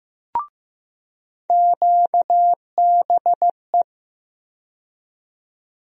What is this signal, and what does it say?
Morse code, heard as a keyed tone, about 15 wpm, 700 Hz. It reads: QBE
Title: USA Call Sign Postfixes - 3 Letters 15wpm